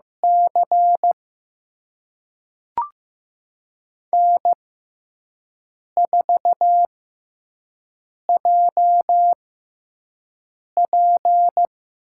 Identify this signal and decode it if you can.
Morse code, heard as a keyed tone, about 15 wpm, spaced as for 5 wpm, 700 Hz. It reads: C N4JP
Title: USA Call Signs - 4 Characters - 6x Character Spacing - 15wpm